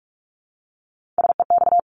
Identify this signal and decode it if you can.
Morse code, heard as a keyed tone, about 46 wpm, 690 Hz. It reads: SEX
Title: Multi-letter Phonemes Repeated 3x 45wpm